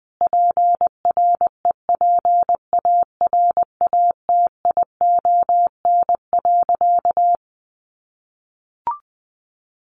Morse code, heard as a keyed tone, about 20 wpm, 700 Hz. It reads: PREPARATION.
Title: Sentences from Top 1400 Words 20wpm